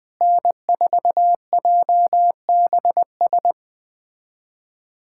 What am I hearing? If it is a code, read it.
Morse code, heard as a keyed tone, about 20 wpm, 700 Hz. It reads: N4JBS
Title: USA Callsign DE Callsign - 5 Characters - 20wpm